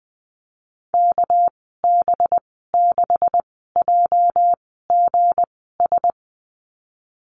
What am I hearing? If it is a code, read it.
Morse code, heard as a keyed tone, about 20 wpm, spaced as for 15 wpm, 700 Hz. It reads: KB6JGS
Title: 213 Lesson 34g - Number 8 - Call Signs 20wpm